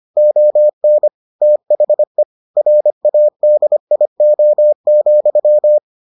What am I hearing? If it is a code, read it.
Morse code, heard as a keyed tone, about 25 wpm, 600 Hz. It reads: ON THE RADIO,